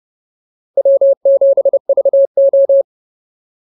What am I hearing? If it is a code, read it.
Morse code, heard as a keyed tone, about 30 wpm, 550 Hz. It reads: W7VO